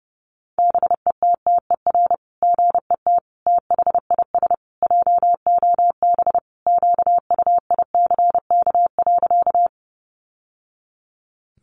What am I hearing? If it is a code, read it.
Morse code, heard as a keyed tone, about 30 wpm, 700 Hz. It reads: BETTER GET THIS JOB QUICK.
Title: Sentences from Top 600 Words 30wpm